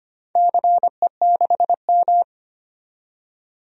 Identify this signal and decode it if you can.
Morse code, heard as a keyed tone, about 25 wpm, 700 Hz. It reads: CE6M
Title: DXCC-100 Call Signs - 4 Characters - 25wpm